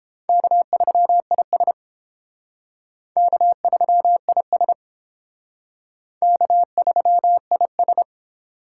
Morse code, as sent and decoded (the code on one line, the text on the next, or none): -.- ...-- .. ... / -.- ...-- .. ... / -.- ...-- .. ...
K3IS K3IS K3IS